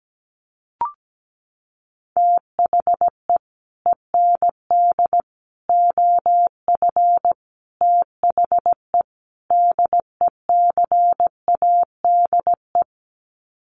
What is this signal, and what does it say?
Morse code, heard as a keyed tone, about 17 wpm, 700 Hz. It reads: THE END OF THE DECADE
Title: Sets of 5 Words Encore 17wpm